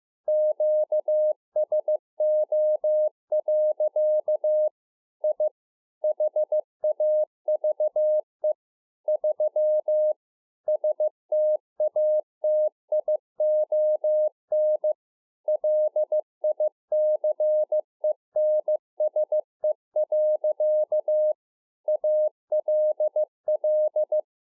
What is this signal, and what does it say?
Morse code, heard as a keyed tone, about 15 wpm, 610 Hz. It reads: QSO. I HAVE 3 STATION LICENSE. ALL